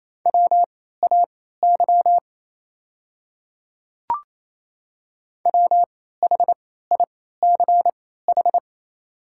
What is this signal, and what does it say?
Morse code, heard as a keyed tone, about 28 wpm, spaced as for 16 wpm, 700 Hz. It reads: WAY WHICH
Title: 3x Character Spacing - Top 100 Words - 28wpm